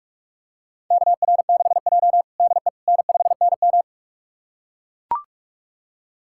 Morse code, heard as a keyed tone, about 42 wpm, 700 Hz. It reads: KR6J DE N5NM
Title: USA Callsign DE Callsign - 4 Characters - 45wpm